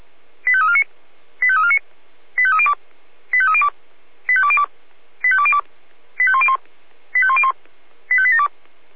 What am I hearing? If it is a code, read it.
Five-tone selective call (ZVEI), heard as quick runs of five tones, tones of about 70 ms. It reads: 85349 85349 85302 85302 85202 85202 85101 85101 85872